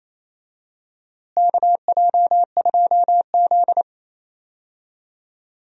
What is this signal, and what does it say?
Morse code, heard as a keyed tone, about 27 wpm, 700 Hz. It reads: KJ2Z
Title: USA Call Signs - 4 Characters - Repeated 2x - 28wpm